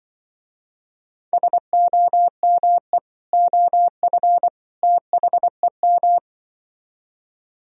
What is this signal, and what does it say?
Morse code, heard as a keyed tone, about 25 wpm, 700 Hz. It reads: SOME OF THEM
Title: Speed-Racing - Sets of 3 Words 20wpm